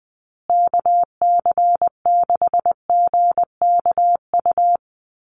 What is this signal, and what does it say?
Morse code, heard as a keyed tone, about 20 wpm, 700 Hz. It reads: KC6GKU
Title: USA Call Signs Repeated 3x 20wpm